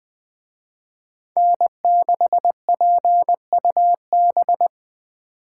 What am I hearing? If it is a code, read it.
Morse code, heard as a keyed tone, about 20 wpm, 700 Hz. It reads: N6PUB